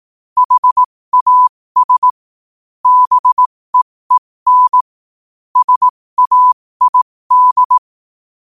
Morse code, heard as a keyed tone, about 18 wpm, spaced as for 15 wpm, 1000 Hz. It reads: HAS BEEN SAID
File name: qotdfiles / 15wpm